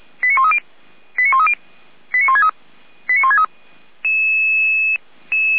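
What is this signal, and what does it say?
Five-tone selective call (ZVEI), heard as quick runs of five tones, tones of about 70 ms. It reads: 89130 89130 89163 89163
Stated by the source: ZVEI five-tone selective call received on an SDR.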